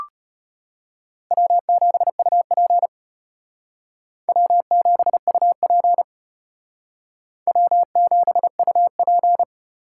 Morse code, heard as a keyed tone, about 32 wpm, 700 Hz. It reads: W7UP W7UP W7UP